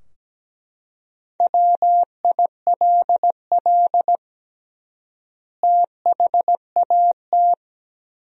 Morse code, heard as a keyed tone, about 17 wpm, 700 Hz. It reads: WILL THAT
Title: 3x Word-Spacing - Sentences from Top 100 Words - 17wpm